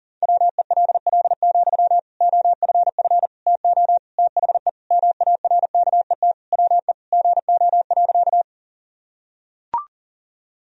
Morse code, heard as a keyed tone, about 40 wpm, 700 Hz. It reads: WELL, OFF TO THE MARKET WE GO.
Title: Sentences from Top 1000 Words - Review Part 3 - 40wpm